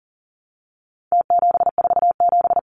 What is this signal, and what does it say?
Morse code, heard as a keyed tone, about 40 wpm, 700 Hz. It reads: T747
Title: Numbers - Contest Style - 4-Digit 40wpm